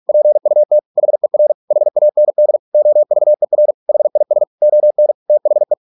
Morse code, heard as a keyed tone, about 46 wpm, 600 Hz. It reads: PUT HER HAND OVER HIS ON THE